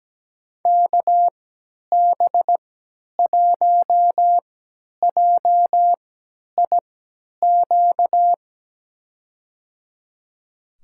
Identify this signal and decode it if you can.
Morse code, heard as a keyed tone, about 17 wpm, spaced as for 10 wpm, 700 Hz. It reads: KB1JIQ
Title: USA Call Signs - 6 Characters - 3x Character Spacing - 17wpm